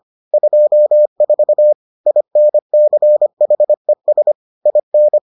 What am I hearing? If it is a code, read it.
Morse code, heard as a keyed tone, about 25 wpm, 600 Hz. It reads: 24 INCHES IN